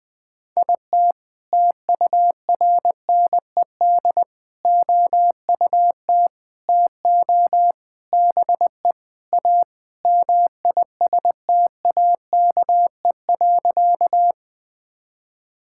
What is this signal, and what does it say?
Morse code, heard as a keyed tone, about 20 wpm, 700 Hz. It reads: IT TURNED OUT TO BE A MISTAKE.
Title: Sentences with 4-Word Phrases - 20wpm